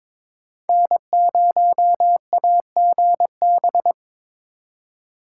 Morse code, heard as a keyed tone, about 26 wpm, 700 Hz. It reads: N0AGB
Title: Speed Racing - USA 1x3 Call Signs 22wpm